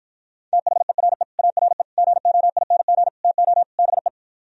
Morse code, heard as a keyed tone, about 52 wpm, 700 Hz. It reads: THERE ARE GOING TO BE